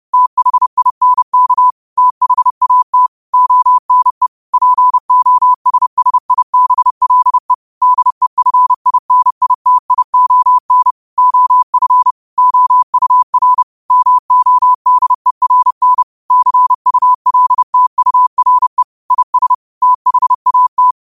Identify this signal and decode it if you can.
Morse code, heard as a keyed tone, about 30 wpm, 1000 Hz. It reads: THINK THAT ONE POSSIBLE DEFINITION OF OUR MODERN CULTURE IS THAT
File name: qotdfiles / 30wpm